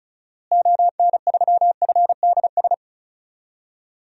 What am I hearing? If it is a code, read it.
Morse code, heard as a keyed tone, about 34 wpm, 700 Hz. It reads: ON3FDS
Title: DXCC-100 Call Signs - 6 Characters - Repeated 2x - 35wpm